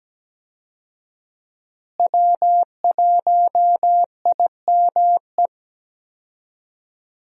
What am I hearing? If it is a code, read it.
Morse code, heard as a keyed tone, about 17 wpm, 700 Hz. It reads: W1IME
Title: USA Call Signs - 5 Characters - Repeated 2x - 17wpm